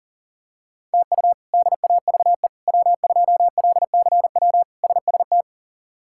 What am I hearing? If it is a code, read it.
Morse code, heard as a keyed tone, about 37 wpm, 700 Hz. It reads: TU DAVE W2LCW SST
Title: K1USN SST QSO Practice - 40wpm